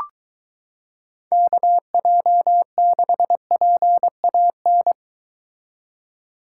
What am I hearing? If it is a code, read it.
Morse code, heard as a keyed tone, about 23 wpm, 700 Hz. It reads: KJ6PAN